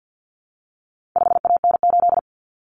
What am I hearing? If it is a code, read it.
Morse code, heard as a keyed tone, about 50 wpm, 700 Hz. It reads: 5AN8
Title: Common Cut Numbers - 4-Digit 50wpm